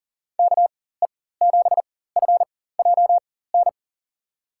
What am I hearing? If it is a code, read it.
Morse code, heard as a keyed tone, about 40 wpm, spaced as for 19 wpm, 700 Hz. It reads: KE7FJN
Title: USA Call Signs - 6 Characters - 4x Character Spacing - 40wpm